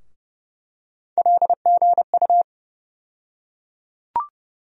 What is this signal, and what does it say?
Morse code, heard as a keyed tone, about 30 wpm, 700 Hz. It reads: LGU
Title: USA Call Sign Postfixes - 3 Letters 30wpm